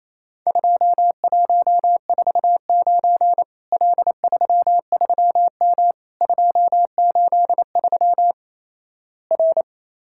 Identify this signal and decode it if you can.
Morse code, heard as a keyed tone, about 27 wpm, 700 Hz. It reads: 2149 L33M 283 R
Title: Worked All Europe DX Contest - QTC Exchange - 28wpm